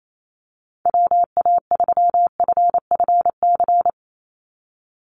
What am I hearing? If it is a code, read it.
Morse code, heard as a keyed tone, about 28 wpm, 700 Hz. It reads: WA3FFC
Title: USA Call Signs 28wpm